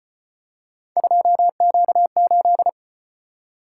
Morse code, heard as a keyed tone, about 32 wpm, 700 Hz. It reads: 2Q8